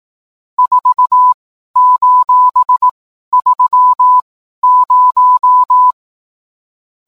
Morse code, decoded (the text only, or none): 4:30